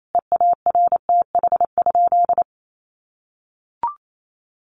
Morse code, heard as a keyed tone, about 28 wpm, 700 Hz. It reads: EARTH?
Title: Sentences from Top 1500 Words 28wpm